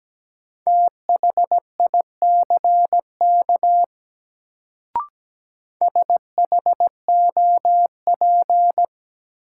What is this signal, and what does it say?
Morse code, heard as a keyed tone, about 17 wpm, 700 Hz. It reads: THICK SHOP
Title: Sentences from Top 1600 Words - New Word Drill - 17wpm